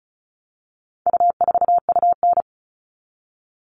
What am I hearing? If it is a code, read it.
Morse code, heard as a keyed tone, about 34 wpm, 700 Hz. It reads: U4UN
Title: All Cut Numbers - 4-Digit 35wpm